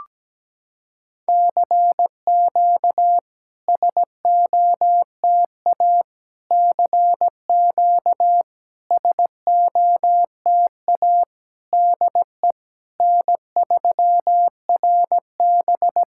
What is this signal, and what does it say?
Morse code, heard as a keyed tone, about 17 wpm, 700 Hz. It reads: CQ SOTA CQ SOTA DE N3RB
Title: SOTA QSO Practice 17wpm